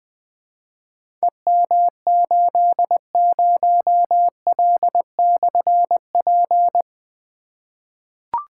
Morse code, heard as a keyed tone, about 21 wpm, 700 Hz. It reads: EM80L/P
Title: Difficult Call Signs 20wpm